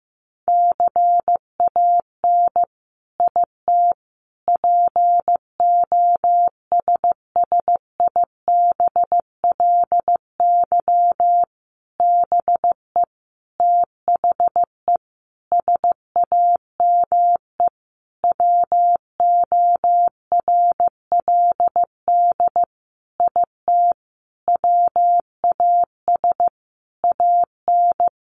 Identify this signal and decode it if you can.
Morse code, heard as a keyed tone, about 15 wpm, 700 Hz. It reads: CAN IT POSSIBLY BE THE SAME WORLD IT WAS AN
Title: Sentences from Top 1300 Words 15wpm